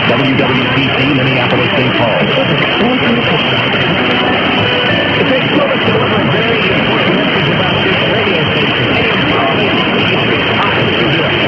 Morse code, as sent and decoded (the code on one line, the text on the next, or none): none